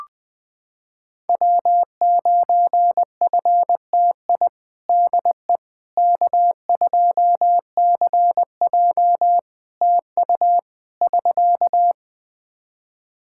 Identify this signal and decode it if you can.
Morse code, heard as a keyed tone, about 20 wpm, 700 Hz. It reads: W9FTI DE K2CJ TU <SK>
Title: K1USN SST QSO Practice 20wpm